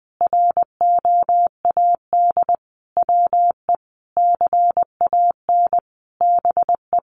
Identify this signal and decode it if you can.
Morse code, heard as a keyed tone, about 19 wpm, 700 Hz. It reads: ROAD WE CAN BE
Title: Sentences from Top 1100 Words 20wpm